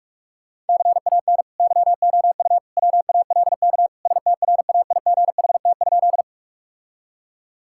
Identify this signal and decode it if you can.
Morse code, heard as a keyed tone, about 41 wpm, 700 Hz. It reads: CAN YOU WALK STRAIGHT?
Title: Sentences from Top 1000 Words - Review Part 6 - 45wpm